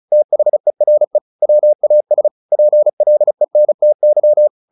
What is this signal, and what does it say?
Morse code, heard as a keyed tone, about 37 wpm, 600 Hz. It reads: THERE WAS PLENTY
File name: the old man and the sea 2 35wpm